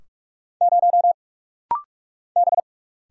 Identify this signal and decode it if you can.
Morse code, heard as a keyed tone, about 44 wpm, 700 Hz. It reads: 0 B
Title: Single Letter-Number - Rapid-Fire 45wpm